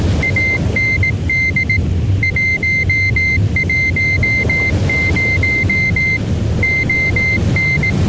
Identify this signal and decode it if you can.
Morse code, heard as a keyed tone, about 18 wpm, 2100 Hz. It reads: AND 110 ON